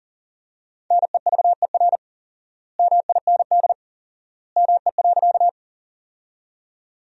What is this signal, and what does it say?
Morse code, heard as a keyed tone, about 37 wpm, 700 Hz. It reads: NEVER MIND ME.
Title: 4x Word-Spacing - Sentences from Top 500 Words - 40wpm